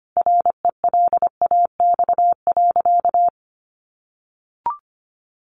Morse code, heard as a keyed tone, about 25 wpm, 700 Hz. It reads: RELAX.
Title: A Sound Of Thunder 25wpm